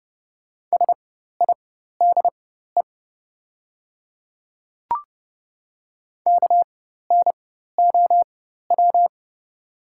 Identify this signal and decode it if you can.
Morse code, heard as a keyed tone, about 30 wpm, spaced as for 14 wpm, 690 Hz. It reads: SIDE KNOW
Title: 4x Character Spacing - Top 100 Words - 30wpm